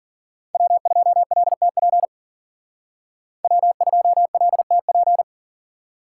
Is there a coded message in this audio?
Yes: Morse code from 0.5 to 5.2 s, heard as a keyed tone, about 52 wpm, 700 Hz. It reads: W2LTP W2LTP